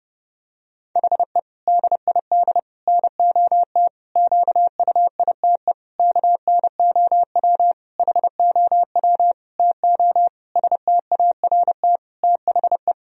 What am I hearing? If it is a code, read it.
Morse code, heard as a keyed tone, about 30 wpm, 700 Hz. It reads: HE DID NOT QUITE KNOW HOW TO START THE
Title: Sentences from Top 1000 Words - Review Part 1 - 30wpm